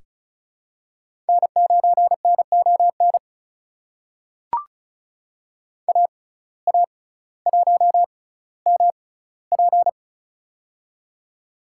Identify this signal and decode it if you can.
Morse code, heard as a keyed tone, about 36 wpm, spaced as for 18 wpm, 700 Hz. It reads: N9NON AA1MP
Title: USA Call Signs - 5 Characters - 6x Character Spacing - 35wpm